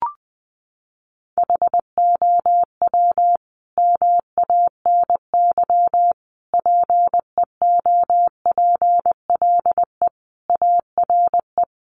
Morse code, heard as a keyed tone, about 20 wpm, 700 Hz. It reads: HOW MANY PEOPLE ARE
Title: Sentences from Top 1200 Words 20wpm